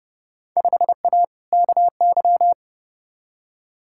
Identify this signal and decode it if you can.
Morse code, heard as a keyed tone, about 30 wpm, 700 Hz. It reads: 5A KY
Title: Field Day Exchanges - 30wpm